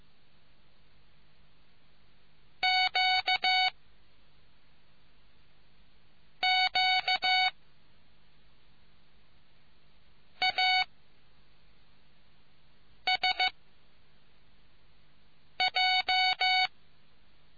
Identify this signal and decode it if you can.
Morse code, heard as a keyed tone, about 15 wpm, spaced as for 3 wpm, 740 Hz. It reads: QQASJ